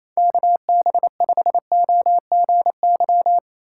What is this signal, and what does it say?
Morse code, heard as a keyed tone, about 35 wpm, 700 Hz. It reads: KB5OGY